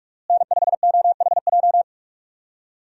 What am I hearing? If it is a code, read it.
Morse code, heard as a keyed tone, about 46 wpm, 700 Hz. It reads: N5OHJ
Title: USA Call Signs - 5 Characters - Repeated 3x - 45wpm